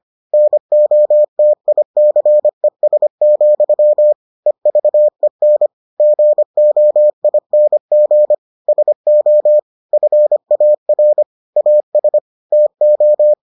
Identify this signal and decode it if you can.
Morse code, heard as a keyed tone, about 25 wpm, 600 Hz. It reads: NOTICES, EVEN GOING SO FAR AS TO